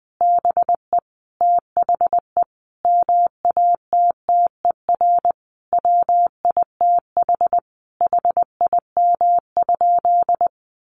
Morse code, heard as a keyed tone, about 20 wpm, 700 Hz. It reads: BE THE MATTER WITH HIM?
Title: Sentences from Top 700 Words 20wpm